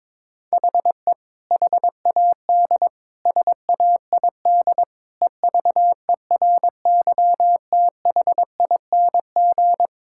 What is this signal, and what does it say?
Morse code, heard as a keyed tone, about 22 wpm, 700 Hz. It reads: HE HAD SAID EVERYTHING
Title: Sentences from Top 1600 Words 22wpm